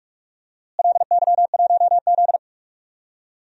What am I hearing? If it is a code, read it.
Morse code, heard as a keyed tone, about 42 wpm, 700 Hz. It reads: LY1Z
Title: DXCC-100 Call Signs - 4 Characters - Repeated 3x - 45wpm